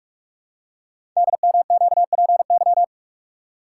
Morse code, heard as a keyed tone, about 44 wpm, 700 Hz. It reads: DMQPY
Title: Code Groups - 5 Letters - 45wpm